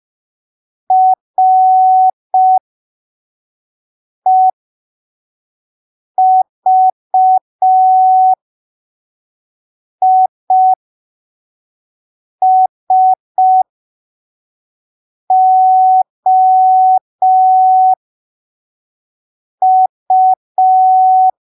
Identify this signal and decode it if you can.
Morse code, heard as a keyed tone, about 5 wpm, spaced as for 3 wpm, 750 Hz. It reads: REVISOU
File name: arquivos / texto / texto 07 05ppm